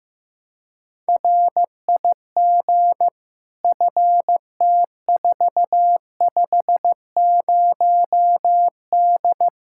Morse code, heard as a keyed tone, about 16 wpm, 700 Hz. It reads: RIG FT450D
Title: Practice QSO - Exchange 2 - Info - 15wpm